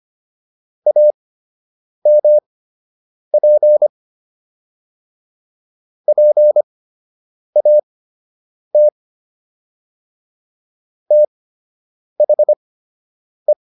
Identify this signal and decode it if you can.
Morse code, heard as a keyed tone, about 25 wpm, spaced as for 8 wpm, 600 Hz. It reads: AMP PAT THE